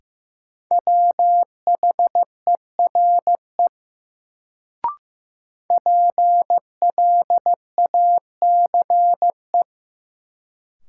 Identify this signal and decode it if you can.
Morse code, heard as a keyed tone, about 15 wpm, 700 Hz. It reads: WHERE PLACE
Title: Sentences from Top 200 Words - New Word Drill - 15wpm